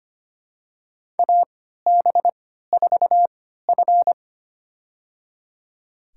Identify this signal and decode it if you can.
Morse code, heard as a keyed tone, about 25 wpm, spaced as for 14 wpm, 700 Hz. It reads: AB4F
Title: USA Call Signs - 4 Characters - 3x Character Spacing - 25wpm